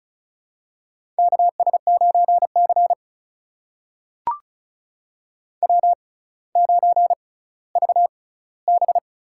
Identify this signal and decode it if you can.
Morse code, heard as a keyed tone, about 35 wpm, spaced as for 18 wpm, 700 Hz. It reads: KS9C W9VB
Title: USA Call Signs - 4 Characters - 6x Character Spacing - 35wpm